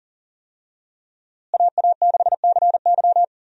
Morse code, heard as a keyed tone, about 40 wpm, 700 Hz. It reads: AA6CY
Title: USA Call Signs - 5 Characters - Repeated 3x - 40wpm